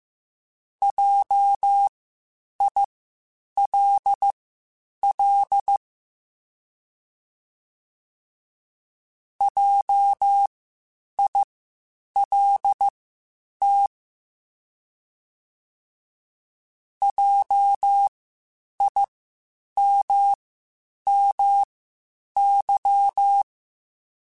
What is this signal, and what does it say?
Morse code, heard as a keyed tone, about 15 wpm, spaced as for 8 wpm, 800 Hz. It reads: JILL JILT JIMMY